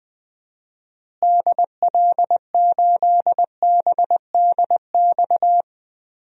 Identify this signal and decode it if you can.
Morse code, heard as a keyed tone, about 21 wpm, 700 Hz. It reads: DL8BDX
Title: DXCC-100 Call Signs - 6 Characters - Repeated 3x - 20wpm